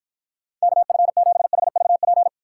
Morse code, heard as a keyed tone, about 52 wpm, 700 Hz. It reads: KF7HVP